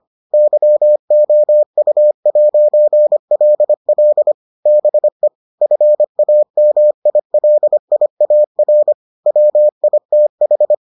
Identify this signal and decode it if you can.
Morse code, heard as a keyed tone, about 25 wpm, 600 Hz. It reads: YOU'LL BE FAMILIAR WITH